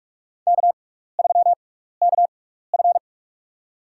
Morse code, heard as a keyed tone, about 47 wpm, spaced as for 16 wpm, 690 Hz. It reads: K3KF
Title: USA Call Signs - 6x Character Spacing - 45wpm